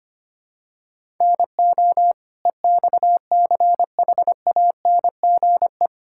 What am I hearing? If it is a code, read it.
Morse code, heard as a keyed tone, about 25 wpm, 700 Hz. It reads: NO EXCHANGE
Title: Sentences from Top 1000 Words - Review Part 6 - 25wpm